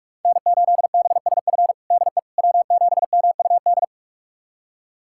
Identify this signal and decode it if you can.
Morse code, heard as a keyed tone, about 45 wpm, 700 Hz. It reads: N8BSF DE W7MUD
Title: USA Callsign DE Callsign - 5 Characters - 45wpm